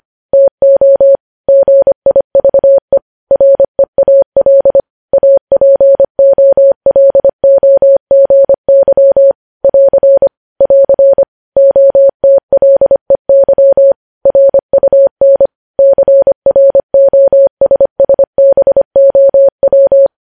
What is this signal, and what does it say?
Morse code, heard as a keyed tone, about 25 wpm, 550 Hz. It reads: TO GIVE REAL APOLOGY <AR> <AR> OTLEY RUN CROSSBOW
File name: podcast / News headlines-1x-25x25